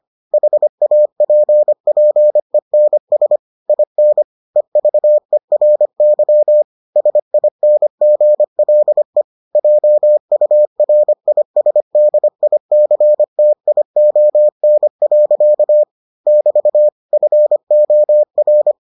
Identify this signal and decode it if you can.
Morse code, heard as a keyed tone, about 25 wpm, 600 Hz. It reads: HAPPENS IN EVERY SINGLE JURISDICTION. <BT> FOR